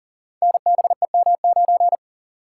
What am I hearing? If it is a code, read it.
Morse code, heard as a keyed tone, about 38 wpm, 700 Hz. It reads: NBEM9
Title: Code Groups - 5 Letter:Numbers - 40wpm